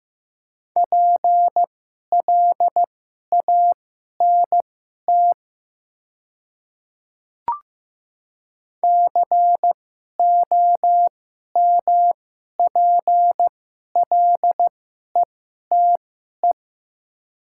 Morse code, heard as a keyed tone, about 15 wpm, spaced as for 11 wpm, 700 Hz. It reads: PLANT COMPLETE
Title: 2x Character Spacing - Top 500 Words - 15wpm